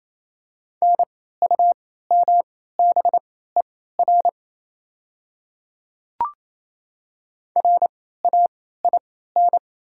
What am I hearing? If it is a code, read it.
Morse code, heard as a keyed tone, about 27 wpm, spaced as for 16 wpm, 690 Hz. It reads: NUMBER RAIN
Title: 3x Character Spacing - Top 300 Words - 28wpm